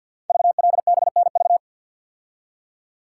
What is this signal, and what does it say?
Morse code, heard as a keyed tone, about 52 wpm, 700 Hz. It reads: VLBNV
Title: Code Groups - 5 Letter:Numbers - 50wpm